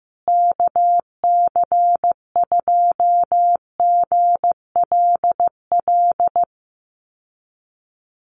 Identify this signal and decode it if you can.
Morse code, heard as a keyed tone, about 15 wpm, 700 Hz. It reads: KC2GLL
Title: USA Call Signs Repeated 3x 15wpm